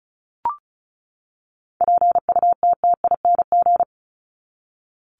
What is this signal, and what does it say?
Morse code, heard as a keyed tone, about 34 wpm, 700 Hz. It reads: PUTTING
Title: ING Suffix - Top 100 Words with Base Words 35wpm